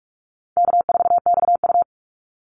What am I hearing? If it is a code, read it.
Morse code, heard as a keyed tone, about 44 wpm, 700 Hz. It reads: K4XU
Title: Easy Call Signs Repeated 2x - 45wpm